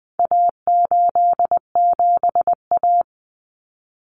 Morse code, heard as a keyed tone, about 20 wpm, 700 Hz. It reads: A87A
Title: Common Cut Numbers - 4-Digit 20wpm